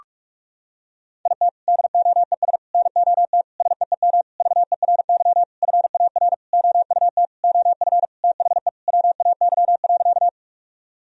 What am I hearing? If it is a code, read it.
Morse code, heard as a keyed tone, about 46 wpm, 700 Hz. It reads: IT DOES NOT SEEM VERY FAR OUT OF THE WAY.
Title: Sentences from Top 1000 Words - Review Part 1 - 45wpm